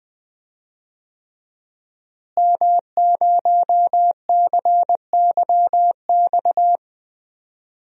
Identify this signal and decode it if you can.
Morse code, heard as a keyed tone, about 20 wpm, 700 Hz. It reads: M0CYX
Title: DXCC-100 Call Signs - 5 Characters - Repeated 3x - 20wpm